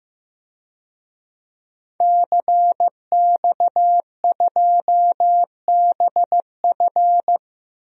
Morse code, heard as a keyed tone, about 15 wpm, 700 Hz. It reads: CX2BF
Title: DXCC-100 Call Signs - 5 Characters - 15wpm